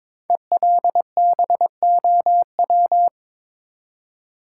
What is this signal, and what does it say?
Morse code, heard as a keyed tone, about 22 wpm, 690 Hz. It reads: ELBOW
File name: 5 Letter Words - First 3 Letters Given - 22wpm